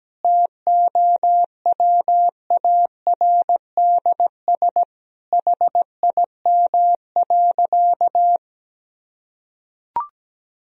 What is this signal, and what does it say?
Morse code, heard as a keyed tone, about 17 wpm, 700 Hz. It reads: TOWARDS HIM.
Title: Sentences from Top 1000 Words - Review Part 2 - 17wpm